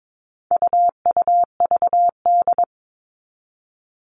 Morse code, heard as a keyed tone, about 22 wpm, 700 Hz. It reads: UUVD
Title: All Cut Numbers - 4-Digit 22wpm